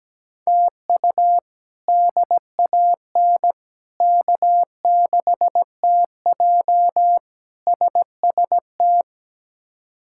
Morse code, heard as a keyed tone, about 17 wpm, 700 Hz. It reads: TU DAN K6TJ SST
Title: K1USN SST QSO Practice - 17wpm